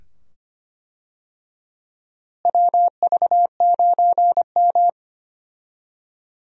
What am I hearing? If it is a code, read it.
Morse code, heard as a keyed tone, about 25 wpm, 700 Hz. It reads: WV9M
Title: USA Call Signs - 4 Characters - Repeated 2x - 25wpm